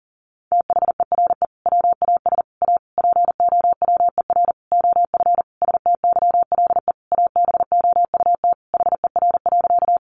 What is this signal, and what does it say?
Morse code, heard as a keyed tone, about 40 wpm, 700 Hz. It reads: THERE WAS A POWER OF STYLE ABOUT HER.
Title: Sentences from Top 800 Words 40wpm